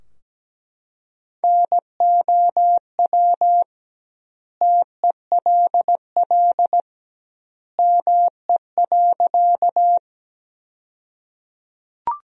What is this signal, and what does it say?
Morse code, heard as a keyed tone, about 17 wpm, 700 Hz. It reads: NOW TELL ME.
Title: 2x Word-Spacing - Sentences from Top 200 Words - 17wpm